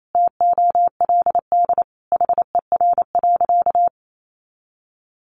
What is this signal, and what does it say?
Morse code, heard as a keyed tone, about 28 wpm, 700 Hz. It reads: TOLD HER.
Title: Sentences from Top 1200 Words 28wpm